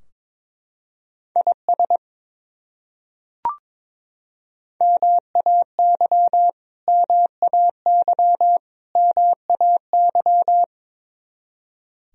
Morse code, heard as a keyed tone, about 21 wpm, 700 Hz. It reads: IS MAY MAY MAY
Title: Top 100 Words Repeated 22wpm